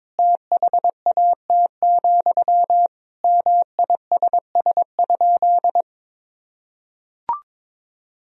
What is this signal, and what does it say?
Morse code, heard as a keyed tone, about 22 wpm, 700 Hz. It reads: THAT, MISS?
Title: Sentences from Top 500 Words and Abbreviations 22wpm